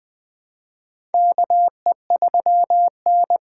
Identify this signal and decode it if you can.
Morse code, heard as a keyed tone, about 20 wpm, 700 Hz. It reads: KE3N
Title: USA Call Signs - 4 Characters - 3x Character Spacing - 20wpm